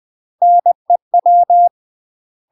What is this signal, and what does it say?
Morse code, heard as a keyed tone, about 20 wpm, 700 Hz. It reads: NEW
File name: UK MASTER QSOs TRAINING / 20WPM / UK MASTER QSOs 0043